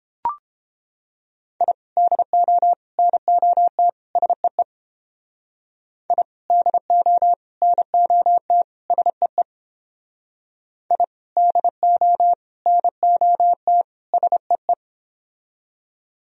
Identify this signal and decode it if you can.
Morse code, heard as a keyed tone, about 31 wpm, 700 Hz. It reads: I DO NOT SEE I DO NOT SEE I DO NOT SEE